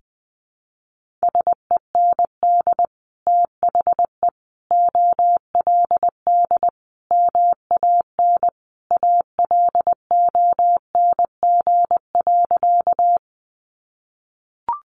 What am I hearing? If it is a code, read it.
Morse code, heard as a keyed tone, about 20 wpm, 700 Hz. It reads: SEND THE OLD MAN ALONG.
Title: Sentences from Top 1500 Words 20wpm